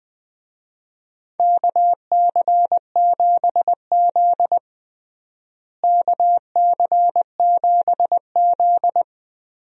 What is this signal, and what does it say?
Morse code, heard as a keyed tone, about 21 wpm, 700 Hz. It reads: KC7Z KC7Z
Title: USA Call Signs - 4 Characters - Repeated 2x - 20wpm